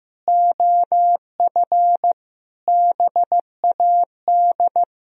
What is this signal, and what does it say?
Morse code, heard as a keyed tone, about 15 wpm, 700 Hz. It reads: OF BAD
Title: Sentences from Top 1000 Words - Review Part 4 - 15wpm